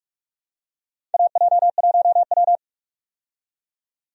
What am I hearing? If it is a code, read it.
Morse code, heard as a keyed tone, about 46 wpm, 700 Hz. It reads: AJ1W
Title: USA Call Signs - 4 Characters - 45wpm